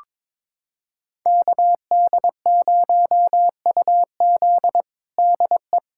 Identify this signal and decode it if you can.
Morse code, heard as a keyed tone, about 22 wpm, 700 Hz. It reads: KD0UZ DE
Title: USA Callsign DE Callsign - 5 Characters - 22wpm